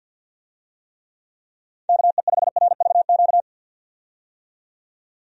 Morse code, heard as a keyed tone, about 52 wpm, 700 Hz. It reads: KE5RVQ
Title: USA Call Signs - 6 Characters - 50wpm